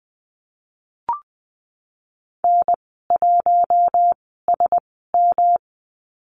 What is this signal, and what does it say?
Morse code, heard as a keyed tone, about 19 wpm, spaced as for 14 wpm, 700 Hz. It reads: N1SM
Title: 193 Lesson 31g - Letter v - Call Signs 20wpm